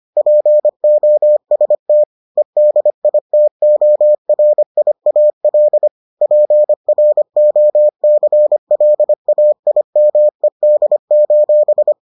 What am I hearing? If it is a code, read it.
Morse code, heard as a keyed tone, about 25 wpm, 600 Hz. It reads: POST EDITORIAL PROCLAIMED: